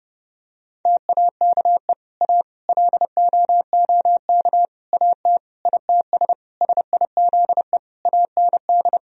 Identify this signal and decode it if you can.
Morse code, heard as a keyed tone, about 30 wpm, 700 Hz. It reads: TAKE A LOOK AT ITS SIZE AND
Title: Sentences from Top 1000 Words - Review Part 2 - 30wpm